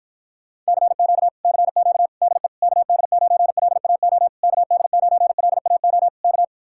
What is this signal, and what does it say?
Morse code, heard as a keyed tone, about 54 wpm, 700 Hz. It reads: CQ CQ DE KD9LAO KD9LAO K